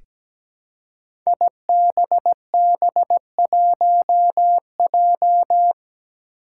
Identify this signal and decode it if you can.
Morse code, heard as a keyed tone, about 17 wpm, 700 Hz. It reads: IBB1J
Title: Five Challenging Letter-Number Combinations 17wpm